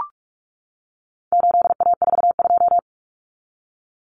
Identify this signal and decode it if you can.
Morse code, heard as a keyed tone, about 44 wpm, 700 Hz. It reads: 8A42
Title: Common Cut Numbers - 4-Digit 45wpm